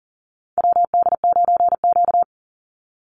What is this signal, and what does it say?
Morse code, heard as a keyed tone, about 38 wpm, 700 Hz. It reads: WD9Q
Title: Call Signs - 40wpm